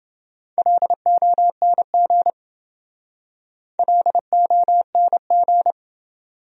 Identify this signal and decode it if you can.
Morse code, heard as a keyed tone, about 27 wpm, 700 Hz. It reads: LONG LONG